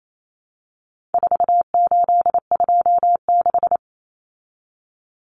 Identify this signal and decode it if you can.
Morse code, heard as a keyed tone, about 28 wpm, 700 Hz. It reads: 4826
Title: Numbers - 4-Digit 28wpm